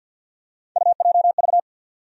Morse code, heard as a keyed tone, about 53 wpm, 700 Hz. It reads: UJV
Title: USA Call Sign Postfixes - 3 Letters 50wpm